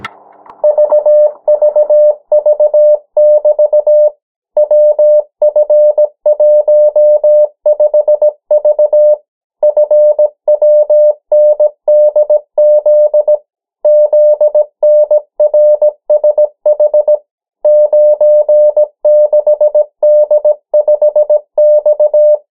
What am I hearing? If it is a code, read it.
Morse code, heard as a keyed tone, about 17 wpm, 600 Hz. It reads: VVV<BT> WF15V FWNDZ ZNRSH 96D5X
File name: morse mp3 / 14 urok 13